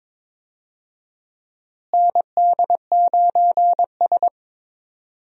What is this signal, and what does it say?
Morse code, heard as a keyed tone, about 22 wpm, 700 Hz. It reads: ND9S
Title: USA Call Signs - 4 Characters - Repeated 3x - 22wpm